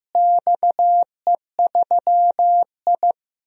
Morse code, heard as a keyed tone, about 15 wpm, 700 Hz. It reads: XE3I
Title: DXCC-100 Call Signs - 4 Characters - Repeated 3x - 15wpm